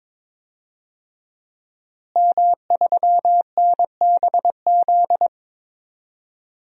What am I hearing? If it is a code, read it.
Morse code, heard as a keyed tone, about 22 wpm, 700 Hz. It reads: M3NBZ
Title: DXCC-100 Call Signs - 5 Characters - Repeated 2x - 22wpm